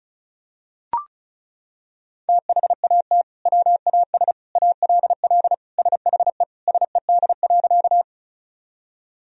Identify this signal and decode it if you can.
Morse code, heard as a keyed tone, about 36 wpm, 700 Hz. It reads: THAT WAS ALL SHE SED.
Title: Sentences from Top 100 Words and Abbreviations 35wpm